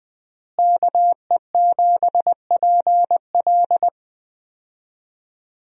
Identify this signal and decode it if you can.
Morse code, heard as a keyed tone, about 20 wpm, 700 Hz. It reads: KE7PL
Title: USA Call Signs - 5 Characters - Repeated 2x - 20wpm